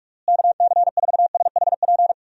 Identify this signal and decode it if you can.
Morse code, heard as a keyed tone, about 46 wpm, 700 Hz. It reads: KC4SHP
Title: USA Call Signs - 6 Characters - Repeated 3x - 45wpm